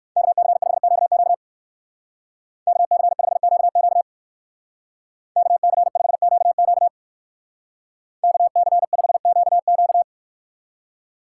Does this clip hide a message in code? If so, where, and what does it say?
Yes: Morse code from 0.2 to 10.0 s, heard as a keyed tone, about 64 wpm, 700 Hz. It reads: KC5QQ KC5QQ KC5QQ KC5QQ